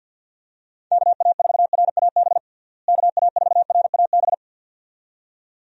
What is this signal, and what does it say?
Morse code, heard as a keyed tone, about 51 wpm, 700 Hz. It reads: KA4RAB KA4RAB
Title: USA Call Signs - 6 Characters - Repeated 2x - 50wpm